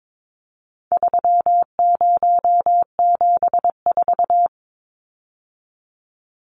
Morse code, heard as a keyed tone, about 22 wpm, 700 Hz. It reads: 3074
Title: Numbers - 4-Digit 22wpm